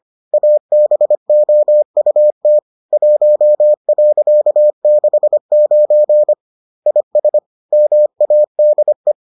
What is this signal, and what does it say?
Morse code, heard as a keyed tone, about 25 wpm, 600 Hz. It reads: ABOUT 1.69 IS MADE